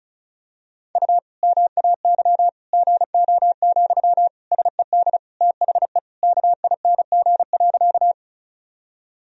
Morse code, heard as a keyed tone, about 36 wpm, 700 Hz. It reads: U MAY GO, SED THE KING.
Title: Sentences from Top 400 Words and Abbreviations 35wpm